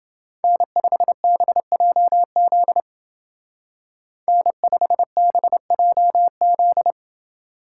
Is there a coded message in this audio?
Yes: Morse code from 0.4 to 6.9 s, heard as a keyed tone, about 29 wpm, 700 Hz. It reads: N5BJZ N5BJZ